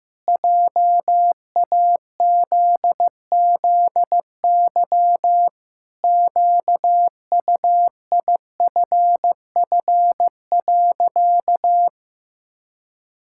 Morse code, heard as a keyed tone, about 16 wpm, 700 Hz. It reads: JAZZY QUIFF.
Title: Pangram Challenge v2 - 15wpm